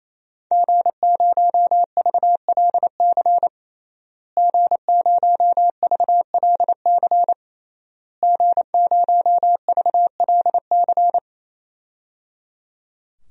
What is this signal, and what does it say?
Morse code, heard as a keyed tone, about 28 wpm, 700 Hz. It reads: G0VLC G0VLC G0VLC
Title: DXCC-100 Call Signs - 5 Characters - Repeated 3x - 28wpm